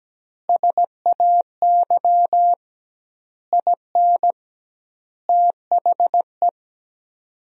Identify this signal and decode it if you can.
Morse code, heard as a keyed tone, about 17 wpm, 700 Hz. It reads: SAY IN THE
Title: 2x Word-Spacing - Sentences from Top 300 Words - 17wpm